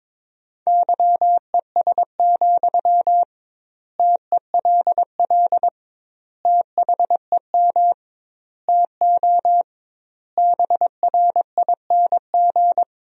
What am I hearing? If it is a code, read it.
Morse code, heard as a keyed tone, about 22 wpm, 700 Hz. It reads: YES, TELL THEM TO BRING
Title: 2x Word-Spacing - Sentences from Top 500 Words - 22wpm